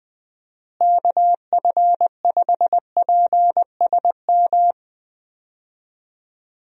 Morse code, heard as a keyed tone, about 20 wpm, 700 Hz. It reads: KF5PSM
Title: USA Call Signs - 6 Characters - Repeated 2x - 20wpm